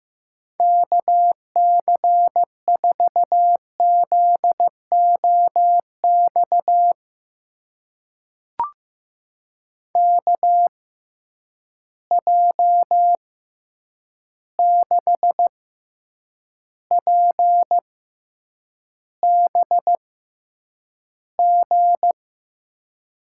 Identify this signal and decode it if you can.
Morse code, heard as a keyed tone, about 15 wpm, spaced as for 8 wpm, 700 Hz. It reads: KC4ZOX KJ6PBG
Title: USA Call Signs - 6 Characters - 6x Character Spacing - 15wpm